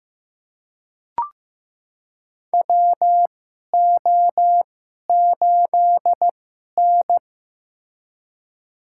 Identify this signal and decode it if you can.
Morse code, heard as a keyed tone, about 15 wpm, spaced as for 11 wpm, 700 Hz. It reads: WO8N
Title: USA Call Signs - 4 Characters - 2x Character Spacing - 15wpm